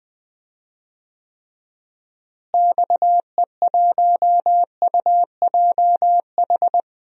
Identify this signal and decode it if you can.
Morse code, heard as a keyed tone, about 20 wpm, 700 Hz. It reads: XE1UJH
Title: DXCC-100 Call Signs - 6 Characters - Repeated 3x - 20wpm